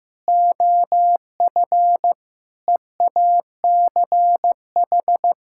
Morse code, heard as a keyed tone, about 15 wpm, 700 Hz. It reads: OF EACH
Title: Sentences from Top 1600 Words 15wpm